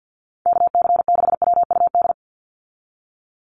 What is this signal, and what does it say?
Morse code, heard as a keyed tone, about 50 wpm, 700 Hz. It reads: KC6WUD
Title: USA Call Signs 50wpm